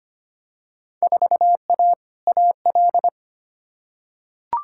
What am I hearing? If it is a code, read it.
Morse code, heard as a keyed tone, about 25 wpm, 700 Hz. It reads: 4A AL
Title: Field Day Exchanges - 25wpm